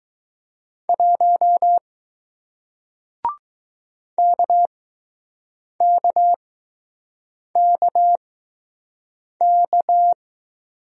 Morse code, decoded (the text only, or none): 1 KKKK